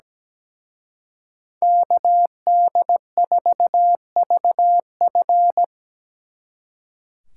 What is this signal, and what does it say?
Morse code, heard as a keyed tone, about 20 wpm, 700 Hz. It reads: KD4VF
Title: Speed Racing - USA 2x2 Call Signs 17wpm